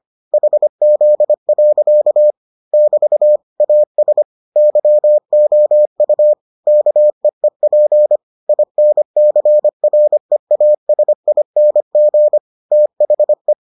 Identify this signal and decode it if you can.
Morse code, heard as a keyed tone, about 25 wpm, 600 Hz. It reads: HZ. <BT> AS YOU KEEP INCREASING THE